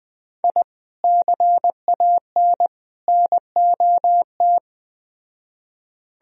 Morse code, heard as a keyed tone, about 20 wpm, 700 Hz. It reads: I CAN NOT